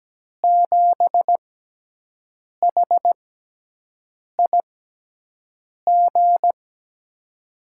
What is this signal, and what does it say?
Morse code, heard as a keyed tone, about 17 wpm, spaced as for 6 wpm, 700 Hz. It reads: 7HIG
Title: USA Call Signs - 6 Characters - 6x Character Spacing - 17wpm